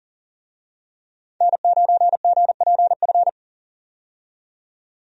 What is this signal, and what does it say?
Morse code, heard as a keyed tone, about 40 wpm, 700 Hz. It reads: N9GPF
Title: USA Call Signs - 5 Characters - 40wpm